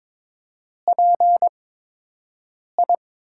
Morse code, heard as a keyed tone, about 22 wpm, 700 Hz. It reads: P I
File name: Top 90 QSO Elements - 8x Character Spacing - 22wpm